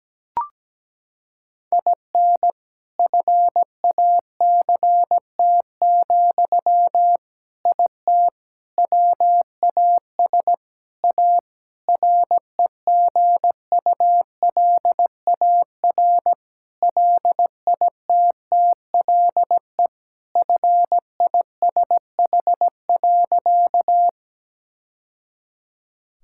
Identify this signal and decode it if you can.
Morse code, heard as a keyed tone, about 17 wpm, 700 Hz. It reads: IN FACT, IT WAS A REGULAR LITTLE FISH.
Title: Sentences from Top 1000 Words - Review Part 6 - 17wpm